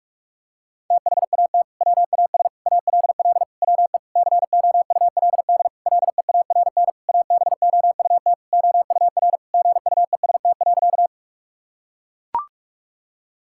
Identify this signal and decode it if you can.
Morse code, heard as a keyed tone, about 42 wpm, 700 Hz. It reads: THAT WAS ALL WE COULD LEARN ABOUT OUR GUEST.
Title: Sentences from Top 1600 Words 45wpm